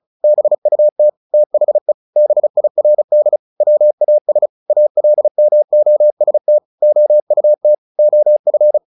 Morse code, heard as a keyed tone, about 35 wpm, 600 Hz. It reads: BUT THE BIRD WAS ALMOST OUT OF
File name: the old man and the sea / the old man and the sea 1 35wpm